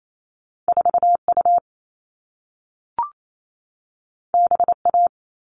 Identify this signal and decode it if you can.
Morse code, heard as a keyed tone, about 28 wpm, 700 Hz. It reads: 4U BA
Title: All Cut Numbers - 2-Digit 28wpm